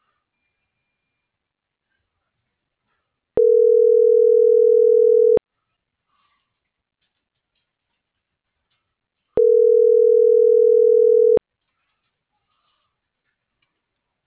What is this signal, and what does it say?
Ringback tone (North America), 440 + 480 Hz.